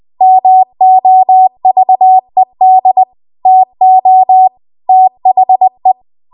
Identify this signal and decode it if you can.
Morse code, heard as a keyed tone, about 20 wpm, 750 Hz. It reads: MOVED TO THE
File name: Morse / 20WPM